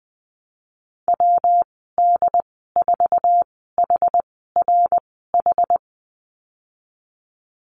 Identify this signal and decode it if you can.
Morse code, heard as a keyed tone, about 20 wpm, spaced as for 15 wpm, 700 Hz. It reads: WD4HRH
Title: 167 Lesson 27g - Number 7 - Call Signs 20wpm